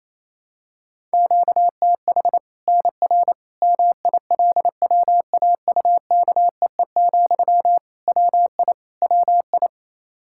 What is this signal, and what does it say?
Morse code, heard as a keyed tone, about 28 wpm, 700 Hz. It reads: QTH NR MILWAUKEE, WI WI
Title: K1USN SST QSO Practice 28wpm